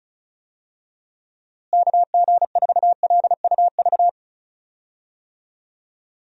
Morse code, heard as a keyed tone, about 36 wpm, 700 Hz. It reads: KG4LUV
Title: USA Call Signs - 6 Characters - Repeated 3x - 35wpm